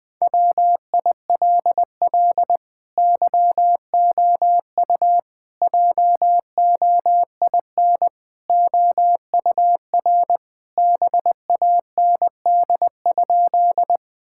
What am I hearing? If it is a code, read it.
Morse code, heard as a keyed tone, about 21 wpm, 700 Hz. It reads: WILL YOU JOIN OUR BAND?
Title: Sentences from Top 1600 Words 20wpm